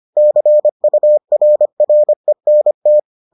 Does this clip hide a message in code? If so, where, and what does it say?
Yes: Morse code from 0.2 to 3.0 s, heard as a keyed tone, about 25 wpm, 600 Hz. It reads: CURRENT